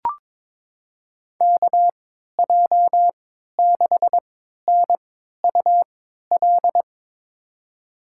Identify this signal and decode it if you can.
Morse code, heard as a keyed tone, about 22 wpm, spaced as for 13 wpm, 700 Hz. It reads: KJ6NUL
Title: USA Call Signs - 6 Characters - 3x Character Spacing - 22wpm